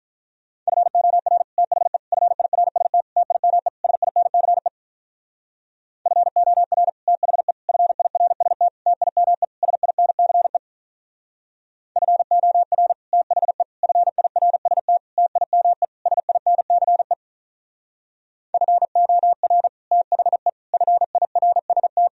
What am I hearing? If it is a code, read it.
Morse code, heard as a keyed tone, about 44 wpm, 700 Hz. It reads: FOR THE FIRST TIME SINCE FOR THE FIRST TIME SINCE FOR THE FIRST TIME SINCE FOR THE FIRST